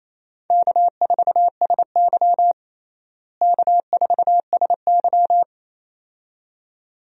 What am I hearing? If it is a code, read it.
Morse code, heard as a keyed tone, about 27 wpm, 700 Hz. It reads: K4SY K4SY
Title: USA Call Signs - 4 Characters - Repeated 2x - 28wpm